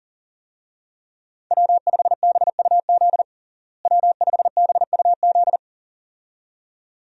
Morse code, heard as a keyed tone, about 40 wpm, 700 Hz. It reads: W5BUZ W5BUZ
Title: USA Call Signs - 5 Characters - Repeated 2x - 40wpm